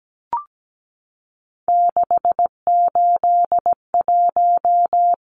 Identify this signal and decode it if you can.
Morse code, heard as a keyed tone, about 17 wpm, 700 Hz. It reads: 681
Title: Numbers - 3-Digit 17wpm